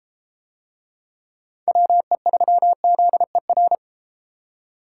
Morse code, heard as a keyed tone, about 31 wpm, 700 Hz. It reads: WE3ZER